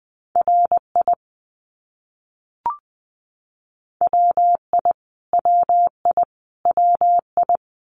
Morse code, heard as a keyed tone, about 20 wpm, 700 Hz. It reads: RI WI WI WI
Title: US States Repeated 3x 20wpm